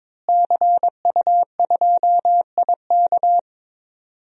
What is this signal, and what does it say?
Morse code, heard as a keyed tone, about 22 wpm, 700 Hz. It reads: CU2IK
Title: DXCC-100 Call Signs - 5 Characters - Repeated 2x - 22wpm